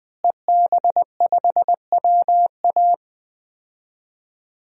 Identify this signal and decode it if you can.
Morse code, heard as a keyed tone, about 20 wpm, 700 Hz. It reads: EB5WA
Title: DXCC-100 Call Signs - 5 Characters - Repeated 3x - 20wpm